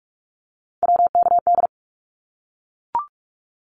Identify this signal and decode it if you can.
Morse code, heard as a keyed tone, about 44 wpm, 700 Hz. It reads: WKD
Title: Abbreviations 45wpm